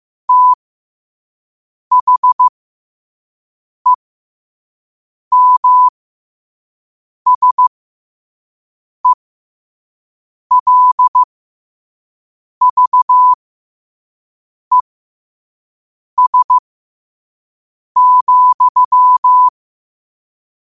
Morse code, heard as a keyed tone, about 15 wpm, spaced as for 5 wpm, 1000 Hz. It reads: THEMSELVES,
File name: qotdfiles / 5wpm